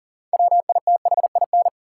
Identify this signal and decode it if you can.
Morse code, heard as a keyed tone, about 40 wpm, 700 Hz. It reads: WITHIN
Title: Six Letter Words - 40wpm